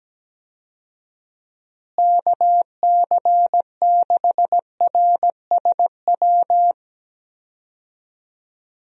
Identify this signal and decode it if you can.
Morse code, heard as a keyed tone, about 17 wpm, 700 Hz. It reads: KC6RSW
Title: USA Call Signs - 6 Characters - Repeated 3x - 17wpm